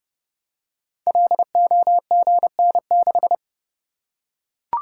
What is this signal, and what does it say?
Morse code, heard as a keyed tone, about 29 wpm, 700 Hz. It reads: LOGN6
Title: Code Groups - 5 Letter:Numbers - 30wpm